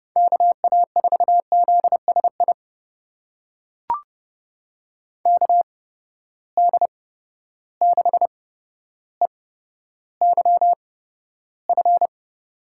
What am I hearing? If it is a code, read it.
Morse code, heard as a keyed tone, about 30 wpm, spaced as for 13 wpm, 700 Hz. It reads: KA4ZSI KD6EYF
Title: USA Call Signs - 6 Characters - 8x Character Spacing - 30wpm